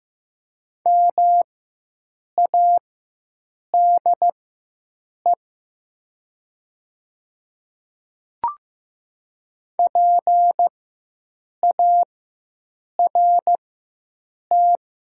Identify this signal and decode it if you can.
Morse code, heard as a keyed tone, about 15 wpm, spaced as for 7 wpm, 700 Hz. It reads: MADE PART
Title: 4x Character Spacing - Top 200 Words - 15wpm